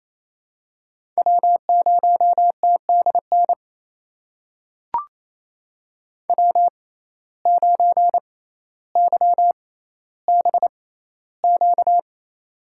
Morse code, heard as a keyed tone, about 28 wpm, spaced as for 14 wpm, 700 Hz. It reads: W0TDN W9YBQ
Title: USA Call Signs - 6x Character Spacing - 28wpm